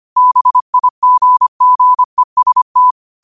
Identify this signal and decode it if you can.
Morse code, heard as a keyed tone, about 25 wpm, 1000 Hz. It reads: BIGGEST
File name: qotdfiles / 25wpm